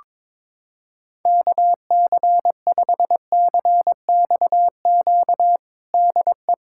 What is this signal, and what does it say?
Morse code, heard as a keyed tone, about 21 wpm, 700 Hz. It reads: KC5CXQ DE
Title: USA Callsign DE Callsign 22wpm